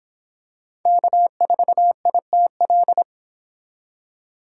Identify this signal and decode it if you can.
Morse code, heard as a keyed tone, about 25 wpm, 700 Hz. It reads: K4ITL